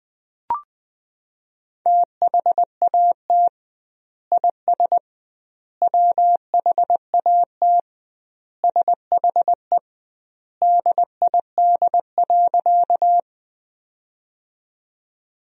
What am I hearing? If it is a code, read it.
Morse code, heard as a keyed tone, about 21 wpm, 700 Hz. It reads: THAT IS WHAT SHE DID.
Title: 2x Word-Spacing - Sentences from Top 100 Words - 20wpm